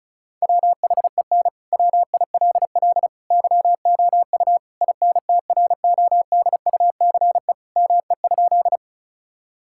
Morse code, heard as a keyed tone, about 34 wpm, 700 Hz. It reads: WHEN WILL YOU INTRODUCE ME?
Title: Sentences from Top 1600 Words 35wpm